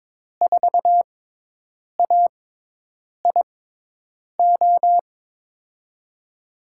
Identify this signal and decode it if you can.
Morse code, heard as a keyed tone, about 22 wpm, spaced as for 8 wpm, 700 Hz. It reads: 4AIO
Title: USA Call Signs - 6 Characters - 6x Character Spacing - 22wpm